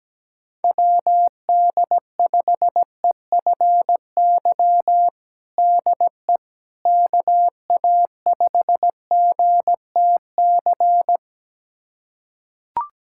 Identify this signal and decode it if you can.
Morse code, heard as a keyed tone, about 17 wpm, 700 Hz. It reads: WD5EFY DE KA5GTC
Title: USA Callsign DE Callsign - 6 Characters - 17wpm